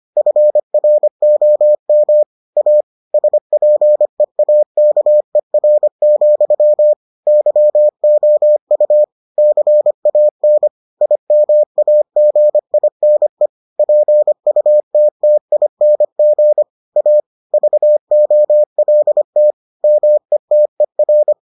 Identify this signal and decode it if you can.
Morse code, heard as a keyed tone, about 25 wpm, 600 Hz. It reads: FROM A SPEAKER, YOU CAN IMAGINE PUTTING A VOLT METER